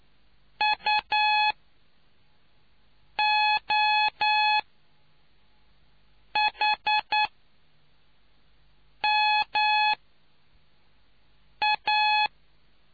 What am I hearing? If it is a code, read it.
Morse code, heard as a keyed tone, about 9 wpm, spaced as for 4 wpm, 850 Hz. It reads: UOHMA